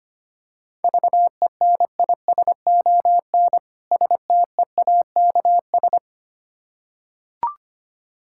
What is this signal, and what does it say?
Morse code, heard as a keyed tone, about 25 wpm, 690 Hz. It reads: VENISON STEAKS
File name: practice / 25wpm / Difficult Words with Hints - 25wpm